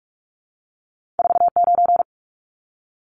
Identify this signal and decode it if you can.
Morse code, heard as a keyed tone, about 46 wpm, 700 Hz. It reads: 49
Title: Numbers - 2-Digit 45wpm